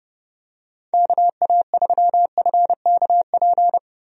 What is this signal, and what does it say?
Morse code, heard as a keyed tone, about 30 wpm, 700 Hz. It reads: KA3FKP
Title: USA Call Signs - 6 Characters - Repeated 3x - 30wpm